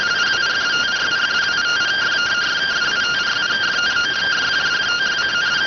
Two-tone teleprinter signal, about 170 Hz shift.